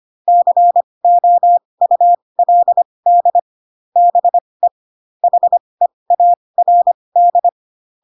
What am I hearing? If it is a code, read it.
Morse code, heard as a keyed tone, about 25 wpm, spaced as for 20 wpm, 700 Hz. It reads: COULD BE HEARD
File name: Pinocchio / Pinocchio ch7 15 25 ik1hge